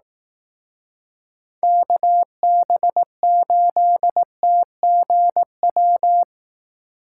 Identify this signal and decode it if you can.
Morse code, heard as a keyed tone, about 18 wpm, 700 Hz. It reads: KB8TGW
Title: Speed Racing - USA 2x3 Call Signs 15wpm